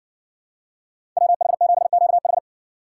Morse code, heard as a keyed tone, about 53 wpm, 700 Hz. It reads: WH7QH